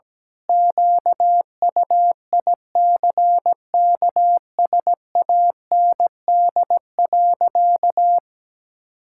Morse code, heard as a keyed tone, about 17 wpm, 700 Hz. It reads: QUICKSAND.
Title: Pangram Challenge v2 - 17wpm